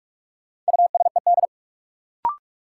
Morse code, heard as a keyed tone, about 41 wpm, 700 Hz. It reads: USED
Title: Sentences from Top 600 Words - New Word Drill - 45wpm